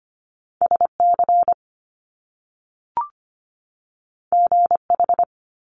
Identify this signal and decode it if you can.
Morse code, heard as a keyed tone, about 25 wpm, 700 Hz. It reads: S C G H
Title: Top Two Letter Combinations in Words 25wpm